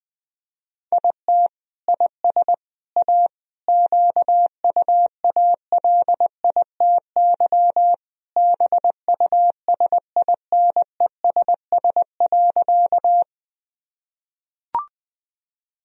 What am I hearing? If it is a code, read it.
Morse code, heard as a keyed tone, about 21 wpm, 700 Hz. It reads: IT IS A QUALITY BUSINESS.
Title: Sentences from Top 1000 Words - Review Part 4 - 20wpm